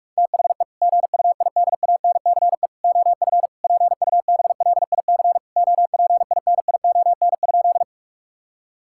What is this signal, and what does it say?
Morse code, heard as a keyed tone, about 41 wpm, 700 Hz. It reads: THE GUIDANCE OF PUBLIC OPINION?
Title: Sentences from Top 1600 Words 45wpm